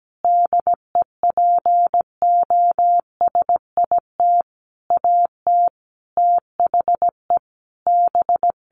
Morse code, heard as a keyed tone, about 17 wpm, 700 Hz. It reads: DEPOSIT AT THE B
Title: Sentences from Top 1400 Words 17wpm